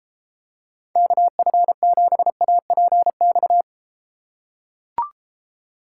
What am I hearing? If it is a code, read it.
Morse code, heard as a keyed tone, about 31 wpm, 700 Hz. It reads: KF7APX